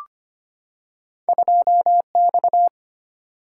Morse code, heard as a keyed tone, about 25 wpm, 700 Hz. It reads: 2X
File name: Two Letter-Number Combinations - 25wpm